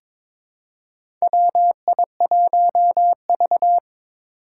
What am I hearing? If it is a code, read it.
Morse code, heard as a keyed tone, about 22 wpm, 700 Hz. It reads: WI1V
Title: USA Call Signs - 4 Characters - 6x Character Spacing - 22wpm